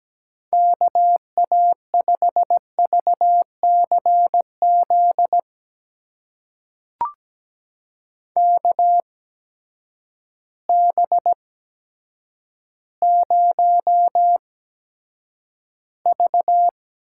Morse code, heard as a keyed tone, about 17 wpm, spaced as for 8 wpm, 700 Hz. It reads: KA5VCZ KB0V
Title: USA Call Signs - 8x Character Spacing - 17wpm